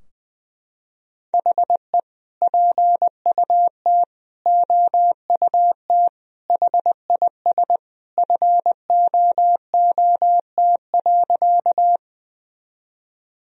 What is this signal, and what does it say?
Morse code, heard as a keyed tone, about 20 wpm, 700 Hz. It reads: HE PUT OUT HIS FOOT.
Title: Sentences from Top 1000 Words - Review Part 3 - 20wpm